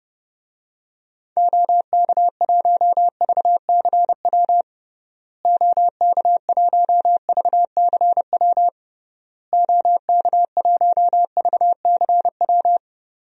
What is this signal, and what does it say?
Morse code, heard as a keyed tone, about 30 wpm, 700 Hz. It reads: OK1VCW OK1VCW OK1VCW
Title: DXCC-100 Call Signs - 6 Characters - Repeated 3x - 30wpm